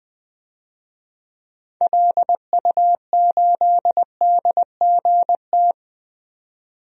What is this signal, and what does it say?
Morse code, heard as a keyed tone, about 20 wpm, 700 Hz. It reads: LU8DGT
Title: DXCC-100 Call Signs - 6 Characters - 20wpm